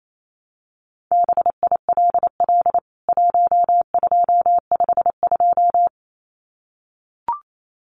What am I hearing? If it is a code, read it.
Morse code, heard as a keyed tone, about 28 wpm, 700 Hz. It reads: BILL 1252
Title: CWT Exchange 28wpm